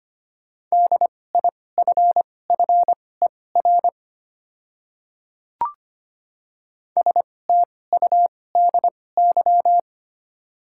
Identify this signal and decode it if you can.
Morse code, heard as a keyed tone, about 25 wpm, spaced as for 18 wpm, 700 Hz. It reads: DIFFER STUDY
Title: 2x Character Spacing - Top 300 Words - 25wpm